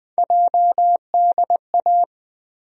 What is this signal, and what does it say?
Morse code, heard as a keyed tone, about 21 wpm, 700 Hz. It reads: JDA
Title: USA Call Sign Postfixes - 3 Letters 20wpm